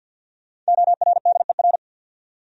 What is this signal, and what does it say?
Morse code, heard as a keyed tone, about 66 wpm, spaced as for 58 wpm, 700 Hz. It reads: ORDER
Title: Speed-Racing Top 500 Words 50wpm